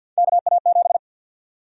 Morse code, heard as a keyed tone, about 52 wpm, 700 Hz. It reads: KA7
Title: USA Call Sign Prefixes - Repeated 2x - 50wpm